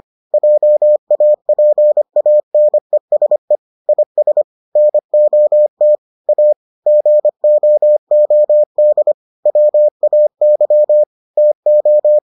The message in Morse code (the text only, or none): JAPANESE IS NOT A GOOD WAY TO